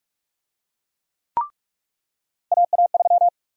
Morse code, heard as a keyed tone, about 44 wpm, 700 Hz. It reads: AA3
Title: USA Call Sign Prefixes - 45wpm